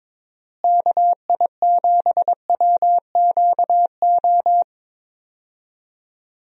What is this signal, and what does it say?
Morse code, heard as a keyed tone, about 22 wpm, 700 Hz. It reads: KI7WQO
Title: USA Call Signs - 6 Characters - Repeated 3x - 22wpm